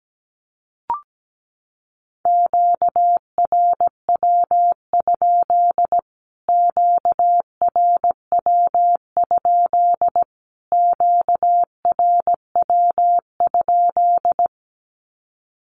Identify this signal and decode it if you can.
Morse code, heard as a keyed tone, about 17 wpm, 700 Hz. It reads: QRW? QRW? QRW?
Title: Q-Codes Repeated 3x 17wpm